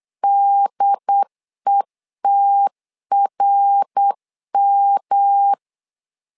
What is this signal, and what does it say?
Morse code, heard as a keyed tone, about 8 wpm, 800 Hz. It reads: DETRM
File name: LEZIONI / 4 Q1S3 8wpm 3037